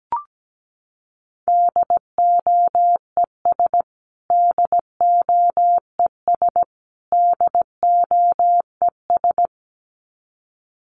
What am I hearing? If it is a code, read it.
Morse code, heard as a keyed tone, about 17 wpm, 700 Hz. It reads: DOES DOES DOES
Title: Top 300 Words Repeated 17wpm